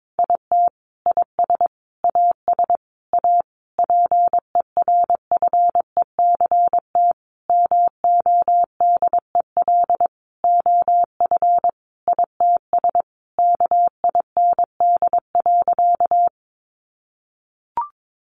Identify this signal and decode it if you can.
Morse code, heard as a keyed tone, about 22 wpm, 700 Hz. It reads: IT IS AS A PERFECT MODEL OF ITS KIND.
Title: Sentences from Top 1000 Words 22wpm